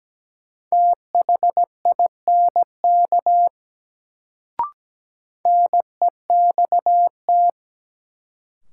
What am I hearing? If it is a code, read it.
Morse code, heard as a keyed tone, about 17 wpm, 700 Hz. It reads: THINK NEXT
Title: Sets of 2 Words - New Word Drill - 17wpm